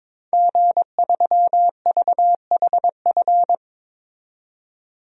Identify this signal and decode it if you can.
Morse code, heard as a keyed tone, about 22 wpm, 700 Hz. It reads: G3VHF
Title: DXCC-100 Call Signs - 5 Characters - 22wpm